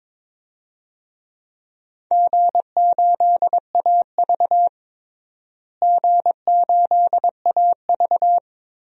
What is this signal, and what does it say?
Morse code, heard as a keyed tone, about 22 wpm, 700 Hz. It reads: G8AV G8AV
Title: DXCC-100 Call Signs - 4 Characters - Repeated 2x - 22wpm